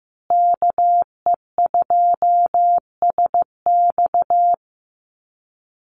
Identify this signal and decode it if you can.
Morse code, heard as a keyed tone, about 15 wpm, 700 Hz. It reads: KE2SX
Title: Call Signs Repeated 2x - 15wpm